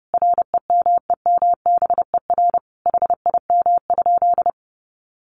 Morse code, heard as a keyed tone, about 30 wpm, 700 Hz. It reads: REMEMBER HIM?
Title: Sentences from Top 600 Words 30wpm